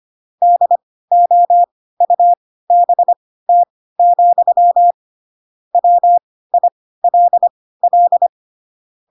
Morse code, heard as a keyed tone, about 25 wpm, spaced as for 16 wpm, 700 Hz. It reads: DOUBT, WILL
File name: Pinocchio / Pinocchio ch3 10 25 ik1hge